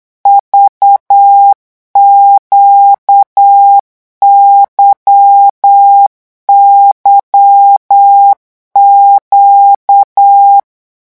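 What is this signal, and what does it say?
Morse code, heard as a keyed tone, about 8 wpm, 800 Hz. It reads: VQYYQ
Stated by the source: Morse code revision lesson at 8 words per minute, from a French amateur radio course.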